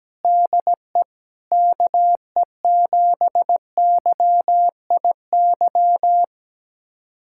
Morse code, heard as a keyed tone, about 17 wpm, 700 Hz. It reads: DE KE7YIY
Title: USA Callsign DE Callsign - 6 Characters - 17wpm